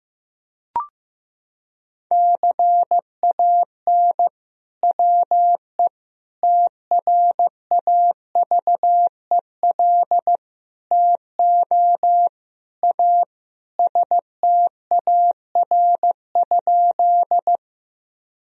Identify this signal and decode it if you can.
Morse code, heard as a keyed tone, about 15 wpm, 700 Hz. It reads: CAN WE TRAVEL TO A STAR?
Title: Sentences from Top 1000 Words - Review Part 2 - 15wpm